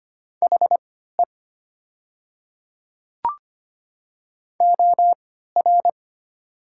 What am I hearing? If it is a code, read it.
Morse code, heard as a keyed tone, about 25 wpm, spaced as for 14 wpm, 700 Hz. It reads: HE OR
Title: 3x Character Spacing - Top 200 Words - 25wpm